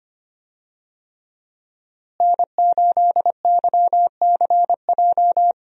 Morse code, heard as a keyed tone, about 25 wpm, 700 Hz. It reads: N8YCJ
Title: USA Call Signs - 5 Characters - Repeated 3x - 25wpm